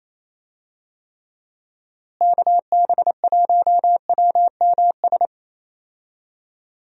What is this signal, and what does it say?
Morse code, heard as a keyed tone, about 28 wpm, 700 Hz. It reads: KB1WMS
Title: USA Call Signs - 6 Characters - Repeated 2x - 28wpm